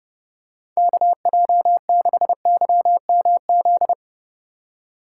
Morse code, heard as a keyed tone, about 30 wpm, 700 Hz. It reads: KJ6YMZ
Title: USA Call Signs - 6 Characters - 2x Character Spacing - 30wpm